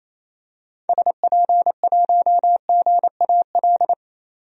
Morse code, heard as a keyed tone, about 28 wpm, 700 Hz. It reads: SP1GAL
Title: DXCC-100 Call Signs - 6 Characters - 28wpm